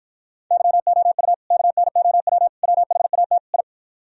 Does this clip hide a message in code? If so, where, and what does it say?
Yes: Morse code from 0.5 to 3.6 s, heard as a keyed tone, about 55 wpm, 700 Hz. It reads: YOU KNOW WHAT I